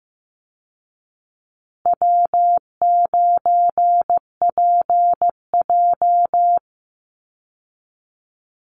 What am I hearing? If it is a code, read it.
Morse code, heard as a keyed tone, about 15 wpm, 700 Hz. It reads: W9PJ
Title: Easy Call Signs Repeated 3x - 15wpm